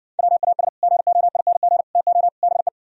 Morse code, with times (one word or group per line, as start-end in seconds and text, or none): 0.2-0.7 WAS
0.8-1.8 GOING
1.9-2.3 TO
2.4-2.7 BE